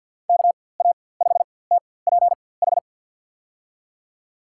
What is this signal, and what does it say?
Morse code, heard as a keyed tone, about 53 wpm, spaced as for 24 wpm, 700 Hz. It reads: KA5TPH
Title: USA Call Signs - 6 Characters - 4x Character Spacing - 50wpm